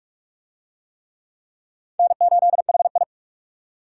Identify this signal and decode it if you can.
Morse code, heard as a keyed tone, about 42 wpm, 700 Hz. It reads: N8HI